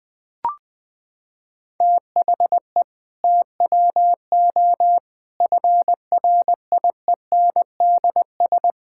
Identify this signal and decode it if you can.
Morse code, heard as a keyed tone, about 20 wpm, 700 Hz. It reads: THE TWO FRIENDS
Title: Sentences with 2-Word Phrases - 20wpm